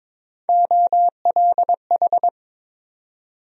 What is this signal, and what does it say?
Morse code, heard as a keyed tone, about 22 wpm, 700 Hz. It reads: OLH
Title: USA Call Sign Postfixes - 3 Letters 22wpm